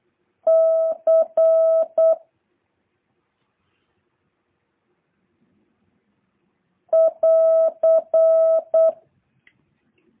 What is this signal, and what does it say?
Morse code, heard as a keyed tone, about 8 wpm, 650 Hz. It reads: C <AR>